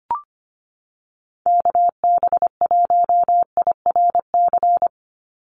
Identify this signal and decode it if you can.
Morse code, heard as a keyed tone, about 25 wpm, 700 Hz. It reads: KB1IRC
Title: USA Call Signs 25wpm